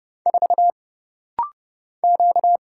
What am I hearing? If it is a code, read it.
Morse code, heard as a keyed tone, about 30 wpm, 700 Hz. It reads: 4 Q
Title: Single Letter-Number Alt - Rapid-Fire 30wpm